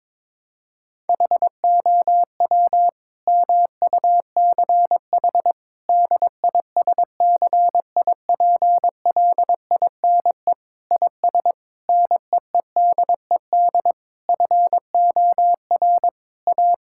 Morse code, heard as a keyed tone, about 22 wpm, 700 Hz. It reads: HOW MUCH DISCIPLINE IS NEEDED FOR A
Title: Sentences from Top 1000 Words - Review Part 6 - 22wpm